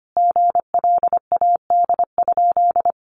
Morse code, heard as a keyed tone, about 25 wpm, 700 Hz. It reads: GLAD?
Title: Sentences from Top 800 Words 25wpm